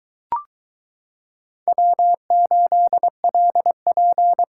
Morse code, heard as a keyed tone, about 22 wpm, 700 Hz. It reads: W8LP